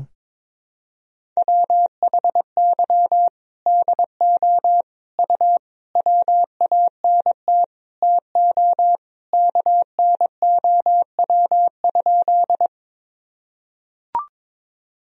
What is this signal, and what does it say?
Morse code, heard as a keyed tone, about 21 wpm, 700 Hz. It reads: WHY DO U WANT TO KNOW?
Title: Sentences from Top 200 Words and Abbreviations 22wpm